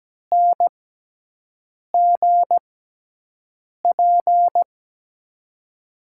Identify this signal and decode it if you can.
Morse code, heard as a keyed tone, about 17 wpm, spaced as for 6 wpm, 700 Hz. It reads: NGP
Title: USA Call Signs - 5 Characters - 6x Character Spacing - 17wpm